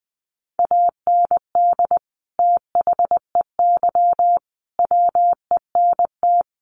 Morse code, heard as a keyed tone, about 20 wpm, 700 Hz. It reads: AND THEY WENT
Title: Sentences from Top 200 Words 20wpm